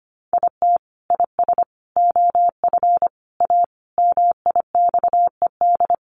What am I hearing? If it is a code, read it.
Morse code, heard as a keyed tone, about 25 wpm, 700 Hz. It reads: IT IS OF A MIXED
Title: Sentences from Top 1500 Words 25wpm